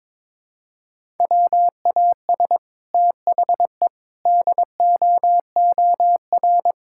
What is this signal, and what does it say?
Morse code, heard as a keyed tone, about 22 wpm, 700 Hz. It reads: WAS THE DOOR
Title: Sentences from Top 1000 Words - Review Part 6 - 22wpm